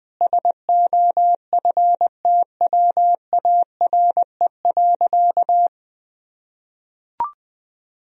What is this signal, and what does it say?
Morse code, heard as a keyed tone, about 20 wpm, 700 Hz. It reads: SOFTWARE.
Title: Sentences from Top 1000 Words - Review Part 3 - 20wpm